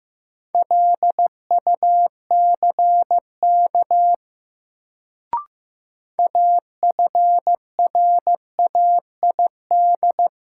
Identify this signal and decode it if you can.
Morse code, heard as a keyed tone, about 15 wpm, 700 Hz. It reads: LUCK AFRAID
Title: Sentences from Top 1500 Words - New Word Drill Alt - 15wpm